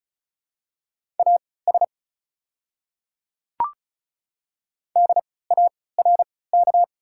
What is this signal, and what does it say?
Morse code, heard as a keyed tone, about 36 wpm, spaced as for 20 wpm, 700 Hz. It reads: AS DARK
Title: 3x Character Spacing - Top 500 Words - 35wpm